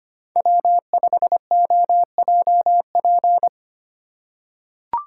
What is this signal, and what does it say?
Morse code, heard as a keyed tone, about 25 wpm, 700 Hz. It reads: W5OJP
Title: USA Call Signs - 5 Characters - 2x Character Spacing - 25wpm